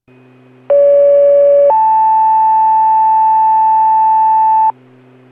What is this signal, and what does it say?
Two-tone page, heard as a one-second tone, then a three-second tone, about 570 then 880 Hz, with noise. It is heard from 0.7 to 4.7 s.